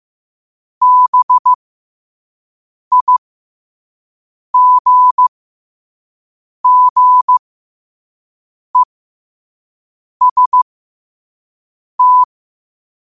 Morse code, heard as a keyed tone, about 15 wpm, spaced as for 5 wpm, 1000 Hz. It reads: BIGGEST
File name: qotdfiles / 5wpm